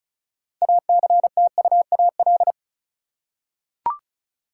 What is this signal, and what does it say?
Morse code, heard as a keyed tone, about 34 wpm, 700 Hz. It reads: ACTUAL
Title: Six Letter Words - 35wpm